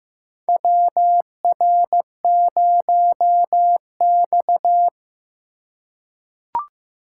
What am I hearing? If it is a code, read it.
Morse code, heard as a keyed tone, about 16 wpm, 700 Hz. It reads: WR0X
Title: USA Call Signs - 4 Characters - 2x Character Spacing - 15wpm